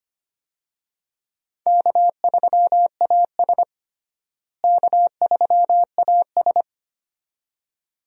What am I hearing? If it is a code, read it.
Morse code, heard as a keyed tone, about 25 wpm, 700 Hz. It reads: K3AS K3AS
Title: USA Call Signs - 4 Characters - Repeated 2x - 25wpm